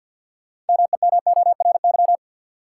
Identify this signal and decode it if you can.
Morse code, heard as a keyed tone, about 53 wpm, 700 Hz. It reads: MEMORY
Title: Sentences from Top 1300 Words - New Word Drill Alt - 50wpm